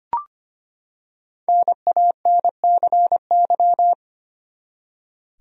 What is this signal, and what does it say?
Morse code, heard as a keyed tone, about 25 wpm, 700 Hz. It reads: NANCY
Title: USA Common Names - Max 5 Letters - 25wpm